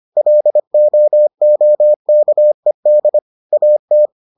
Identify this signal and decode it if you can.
Morse code, heard as a keyed tone, about 25 wpm, 600 Hz. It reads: LOOKED AT